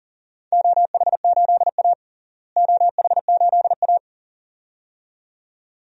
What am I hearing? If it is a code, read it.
Morse code, heard as a keyed tone, about 40 wpm, 700 Hz. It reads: OH8A OH8A
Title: DXCC-100 Call Signs - 4 Characters - Repeated 2x - 40wpm